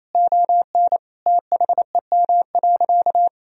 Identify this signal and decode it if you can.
Morse code, heard as a keyed tone, about 28 wpm, 700 Hz. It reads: ON THEM.
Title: Sentences from Top 1000 Words - Review Part 1 - 28wpm